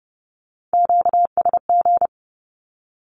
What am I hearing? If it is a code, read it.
Morse code, heard as a keyed tone, about 30 wpm, 690 Hz. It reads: QSG
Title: Q-Codes with Spoken Meaning - 30wpm